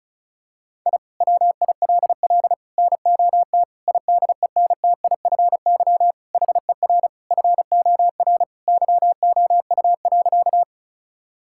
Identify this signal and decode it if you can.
Morse code, heard as a keyed tone, about 34 wpm, 700 Hz. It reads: I WILL NOT IDENTIFY HER FOR YOU.
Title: Sentences from Top 1000 Words - Review Part 6 - 35wpm